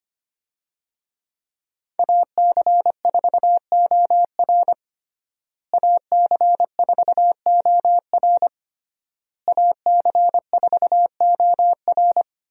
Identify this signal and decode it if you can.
Morse code, heard as a keyed tone, about 25 wpm, 700 Hz. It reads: AC4OR AC4OR AC4OR
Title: USA Call Signs - 5 Characters - Repeated 3x - 25wpm